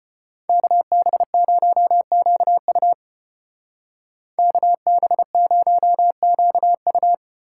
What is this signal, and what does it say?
Morse code, heard as a keyed tone, about 33 wpm, 700 Hz. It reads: KB0QU KB0QU